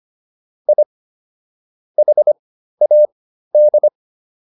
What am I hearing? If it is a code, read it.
Morse code, heard as a keyed tone, about 25 wpm, spaced as for 13 wpm, 600 Hz. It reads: I HAD